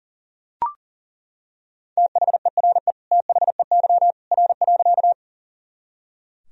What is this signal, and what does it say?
Morse code, heard as a keyed tone, about 40 wpm, 700 Hz. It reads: THERE THEY R.
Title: Sentences from Top 100 Words and Abbreviations 40wpm